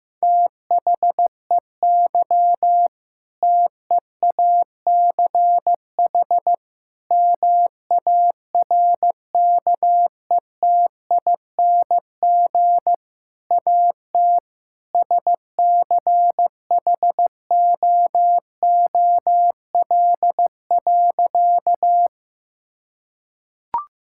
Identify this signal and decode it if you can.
Morse code, heard as a keyed tone, about 16 wpm, 700 Hz. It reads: THEY TEACH MARKETING AT SCHOOL.
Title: Sentences from Top 1000 Words - Review Part 5 - 15wpm